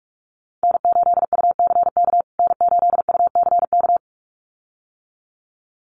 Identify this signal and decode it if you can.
Morse code, heard as a keyed tone, about 44 wpm, 700 Hz. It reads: N8UCK N8UCK
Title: USA Call Signs Repeated 2x 45wpm